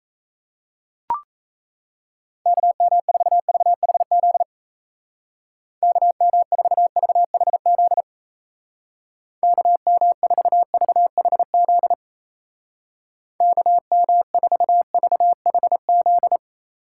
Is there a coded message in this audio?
Yes: Morse code from 2.5 to 16.4 s, heard as a keyed tone, about 36 wpm, 700 Hz. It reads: KM4VHZ KM4VHZ KM4VHZ KM4VHZ